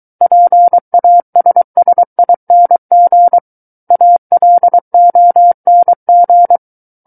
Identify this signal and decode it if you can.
Morse code, heard as a keyed tone, about 23 wpm, 700 Hz. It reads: PASSING ALONG
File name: Book One / 14-Chapter 14 - 23 WPM